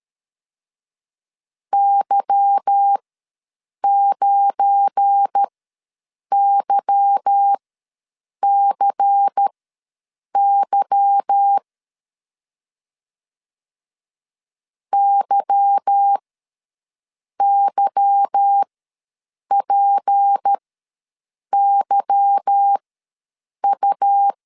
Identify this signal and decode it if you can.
Morse code, heard as a keyed tone, about 13 wpm, spaced as for 7 wpm, 800 Hz. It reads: Y9YCY YYPYU
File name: LEZIONI / 8 YJ45 / 8 Y ws 12wpm 3409